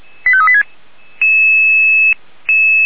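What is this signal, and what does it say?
Five-tone selective call (ZVEI), heard as quick runs of five tones, tones of about 70 ms. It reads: 85386
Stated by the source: ZVEI five-tone selective call received by SDR, from the Feuerwehr Orlamünde fire brigade server.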